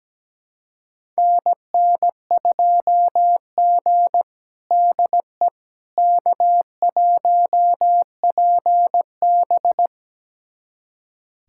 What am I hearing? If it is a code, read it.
Morse code, heard as a keyed tone, about 17 wpm, 700 Hz. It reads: NN2G DE K1PB
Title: USA Callsign DE Callsign - 4 Characters - 17wpm